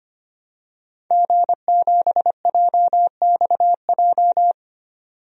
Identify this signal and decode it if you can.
Morse code, heard as a keyed tone, about 25 wpm, 700 Hz. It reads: G7JXJ
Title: DXCC-100 Call Signs - 5 Characters - 25wpm